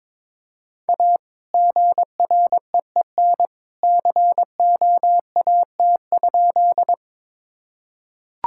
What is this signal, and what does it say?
Morse code, heard as a keyed tone, about 22 wpm, 700 Hz. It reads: A GREEN COAT?
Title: Sentences from Top 1600 Words 22wpm